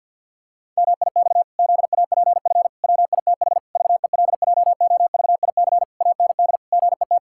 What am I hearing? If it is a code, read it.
Morse code, heard as a keyed tone, about 50 wpm, 700 Hz. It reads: MIX ZAPF WITH VELJOVIC AND GET
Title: Pangram Challenge v2 - 50wpm